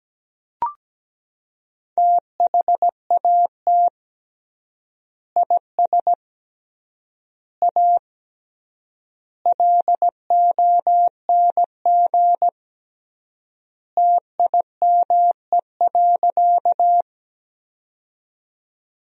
Morse code, heard as a keyed tone, about 17 wpm, 700 Hz. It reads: THAT IS A LONG TIME.
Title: 3x Word-Spacing - Sentences from Top 100 Words - 17wpm